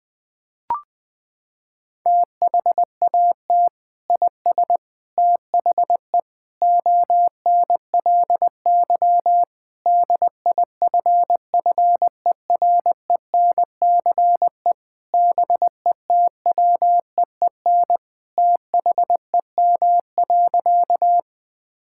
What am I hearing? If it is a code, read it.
Morse code, heard as a keyed tone, about 20 wpm, 700 Hz. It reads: THAT IS THE ONLY DIFFERENCE BETWEEN THEM.
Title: Sentences from Top 1000 Words - Review Part 5 - 20wpm